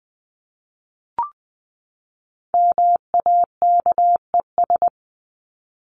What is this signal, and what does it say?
Morse code, heard as a keyed tone, about 20 wpm, 700 Hz. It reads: MAKES
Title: ES:S Suffix - Top 100 Words with Base Words 20wpm